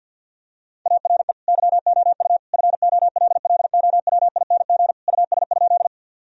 Morse code, heard as a keyed tone, about 50 wpm, 700 Hz. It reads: ARE YOU FOLLOWING US?
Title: Sentences from Top 1000 Words - Review Part 4 - 50wpm